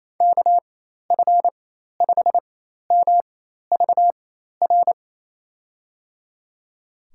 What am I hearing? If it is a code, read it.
Morse code, heard as a keyed tone, about 27 wpm, spaced as for 13 wpm, 700 Hz. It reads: KF5MVR
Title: USA Call Signs - 6 Characters - 4x Character Spacing - 28wpm